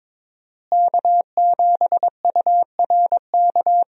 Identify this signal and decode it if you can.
Morse code, heard as a keyed tone, about 22 wpm, 700 Hz. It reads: K7URK
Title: USA Call Signs - 5 Characters - Repeated 2x - 22wpm